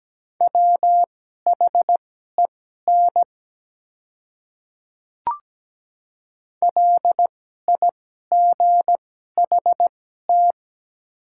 Morse code, heard as a keyed tone, about 17 wpm, spaced as for 12 wpm, 690 Hz. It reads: WHEN LIGHT
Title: 2x Character Spacing - Top 300 Words - 17wpm